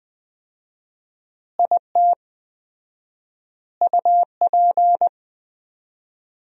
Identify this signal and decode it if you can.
Morse code, heard as a keyed tone, about 19 wpm, 700 Hz. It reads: IT UP
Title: 4x Word-Spacing - Sentences from Top 400 Words - 20wpm